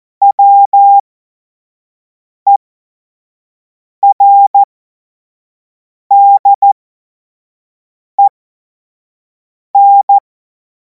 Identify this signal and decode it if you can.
Morse code, heard as a keyed tone, about 14 wpm, spaced as for 5 wpm, 800 Hz. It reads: WERDEN
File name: Morsekurs / Witze / Witz 31. März 2026 25 F70